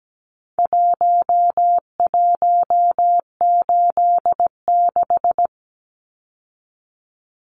Morse code, heard as a keyed tone, about 17 wpm, 700 Hz. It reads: 1186
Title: Numbers - Contest Style - 4-Digit 17wpm